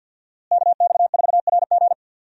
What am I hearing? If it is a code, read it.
Morse code, heard as a keyed tone, about 47 wpm, 700 Hz. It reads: KX4RG
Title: USA Call Signs - 5 Characters - Repeated 2x - 50wpm